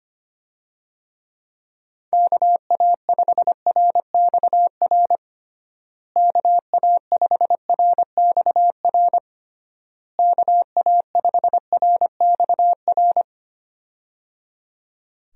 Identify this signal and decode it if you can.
Morse code, heard as a keyed tone, about 25 wpm, 700 Hz. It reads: KA5RXR KA5RXR KA5RXR
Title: USA Call Signs - 6 Characters - Repeated 3x - 25wpm